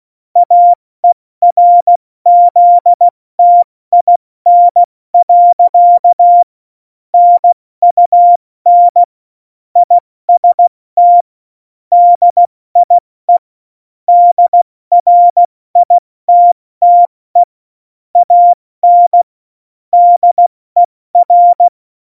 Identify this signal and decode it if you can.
Morse code, heard as a keyed tone, about 16 wpm, 700 Hz. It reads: AERZTIN. NUN IST DIE DRITTE AN DER
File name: Morsekurs / Witze / Witz 19. Januar 2021, 70 F80